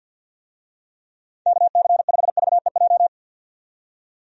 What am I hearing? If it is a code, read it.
Morse code, heard as a keyed tone, about 46 wpm, 700 Hz. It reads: KC5VEJ
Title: USA Call Signs - 6 Characters - 50wpm